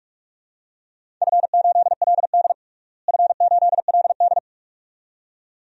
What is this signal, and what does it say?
Morse code, heard as a keyed tone, about 45 wpm, 700 Hz. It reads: F8LD F8LD
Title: DXCC-100 Call Signs - 4 Characters - Repeated 2x - 45wpm